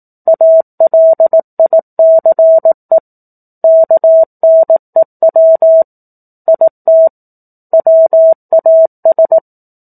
Morse code, heard as a keyed tone, about 18 wpm, 650 Hz. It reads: ALICE KNEW IT WAS
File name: 04-Chapter-04 18 WPM